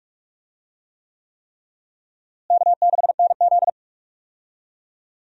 Morse code, heard as a keyed tone, about 46 wpm, 700 Hz. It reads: K6NZ
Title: USA Call Signs - 4 Characters - Repeated 3x - 45wpm